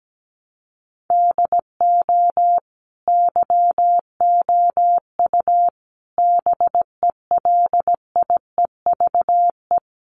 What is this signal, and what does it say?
Morse code, heard as a keyed tone, about 17 wpm, 700 Hz. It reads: DO YOU BELIEVE
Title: Sentences from Top 1000 Words 17wpm